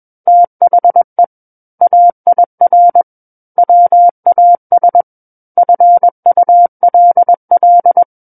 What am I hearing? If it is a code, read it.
Morse code, heard as a keyed tone, about 21 wpm, 700 Hz. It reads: THE AIR WAS FULL
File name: Book One / 12-Chapter 12 - 21 WPM